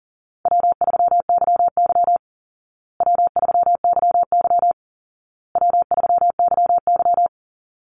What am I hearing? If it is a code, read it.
Morse code, heard as a keyed tone, about 40 wpm, 700 Hz. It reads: W3YY W3YY W3YY
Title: Call Signs Repeated 2x - 40wpm